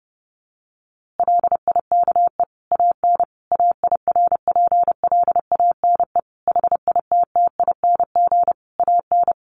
Morse code, heard as a keyed tone, about 30 wpm, 700 Hz. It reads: LIKE AN AIRPLANE HITTING AN
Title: A Sound Of Thunder 30wpm